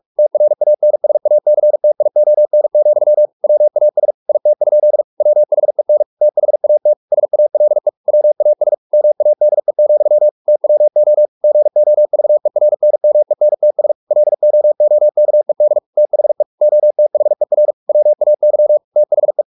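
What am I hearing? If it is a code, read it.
Morse code, heard as a keyed tone, about 44 wpm, 600 Hz. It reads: TRANSACTION, WAS IT? WHEN THAT SALE WAS MADE, TWO GOVERNMENTS LOOKED THE OTHER WAY THE
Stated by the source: Morse practice at 45 wpm, the novel Never by Ken Follett in English, from CW med Gnister.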